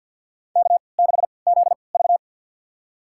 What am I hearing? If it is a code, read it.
Morse code, heard as a keyed tone, about 49 wpm, spaced as for 28 wpm, 700 Hz. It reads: K6ZV
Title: USA Call Signs - 4 Characters - 3x Character Spacing - 50wpm